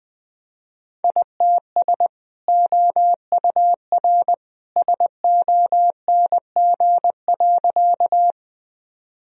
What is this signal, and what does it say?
Morse code, heard as a keyed tone, about 20 wpm, 700 Hz. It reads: ITS OUR SONG.
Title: Sentences from Top 1000 Words - Review Part 2 - 20wpm